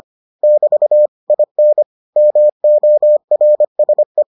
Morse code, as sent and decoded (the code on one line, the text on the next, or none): -...- / .. -. / -- --- .-. ... .
<BT> IN MORSE